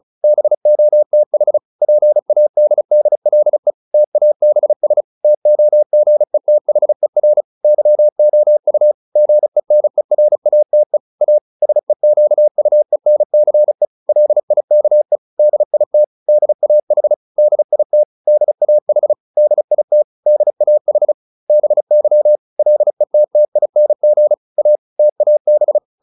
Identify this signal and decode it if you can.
Morse code, heard as a keyed tone, about 35 wpm, 600 Hz. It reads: BOTH PADDLE TABS TOGETHER YOU GENERATE A SEQUENCE LIKE DIT DAH DIT DAH DIT DAH BY LETTING A TAB